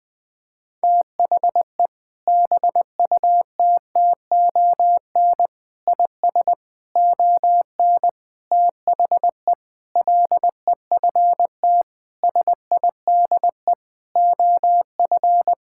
Morse code, heard as a keyed tone, about 20 wpm, 700 Hz. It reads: THE BUTTON IS ON THE LEFT SIDE OF
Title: Sentences with 6-Word Phrases - Spoken Beginning - 20wpm